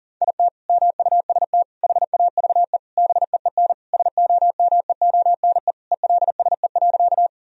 Morse code, heard as a keyed tone, about 37 wpm, 700 Hz. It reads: IT MUST HAVE BEEN SOMEONE ELSE.
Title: Sentences from Top 1000 Words - Review Part 4 - 40wpm